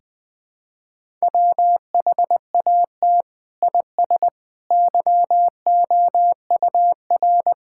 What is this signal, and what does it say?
Morse code, heard as a keyed tone, about 21 wpm, 700 Hz. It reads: WHAT IS YOUR
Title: Sentences from Top 1000 Words - Review Part 5 - 20wpm